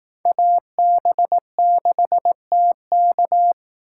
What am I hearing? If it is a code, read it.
Morse code, heard as a keyed tone, about 19 wpm, 700 Hz. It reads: AB6TK
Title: Speed Racing - USA 2x2 Call Signs 15wpm